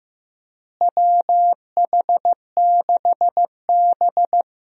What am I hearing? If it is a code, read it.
Morse code, heard as a keyed tone, about 16 wpm, 700 Hz. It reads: WH6B
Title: USA Call Signs - 4 Characters - Repeated 2x - 15wpm